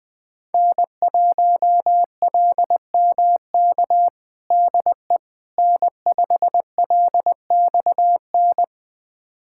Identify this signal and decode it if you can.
Morse code, heard as a keyed tone, about 20 wpm, 700 Hz. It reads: N1LMK DE N5LXN
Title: USA Callsign DE Callsign - 5 Characters - 20wpm